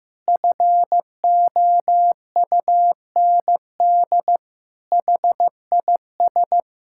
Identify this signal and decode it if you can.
Morse code, heard as a keyed tone, about 15 wpm, 700 Hz. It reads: FOUND HIS
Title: Sentences from Top 1600 Words 15wpm